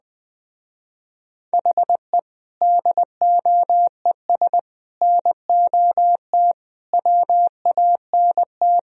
Morse code, heard as a keyed tone, about 21 wpm, 700 Hz. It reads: HE DOES NOT WANT
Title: Speed-Racing - Sets of 5 Words 20wpm